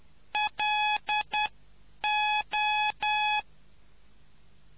Morse code, heard as a keyed tone, about 10 wpm, spaced as for 8 wpm, 840 Hz. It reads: LO